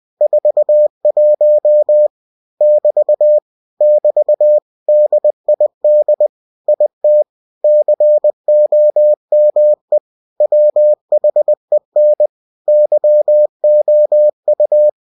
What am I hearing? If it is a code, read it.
Morse code, heard as a keyed tone, about 20 wpm, 600 Hz. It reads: 41 <BT> <BT> DID IT COME WHEN YOU
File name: cw-books / winnie-the-pooh / winnie-20wpm-page-0041